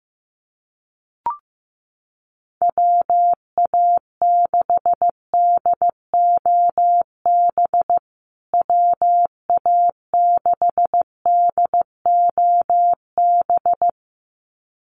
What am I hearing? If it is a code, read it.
Morse code, heard as a keyed tone, about 15 wpm, 700 Hz. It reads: WA6DOB WA6DOB
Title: USA Call Signs Repeated 2x 15wpm